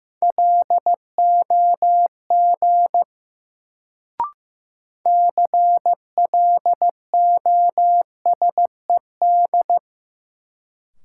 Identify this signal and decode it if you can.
Morse code, heard as a keyed tone, about 15 wpm, 700 Hz. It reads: LOG CLOSED
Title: Sentences from Top 1400 Words - New Word Drill - 15wpm